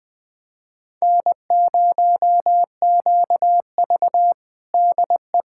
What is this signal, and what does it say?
Morse code, heard as a keyed tone, about 19 wpm, 700 Hz. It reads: N0QV DE
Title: USA Callsign DE Callsign 20wpm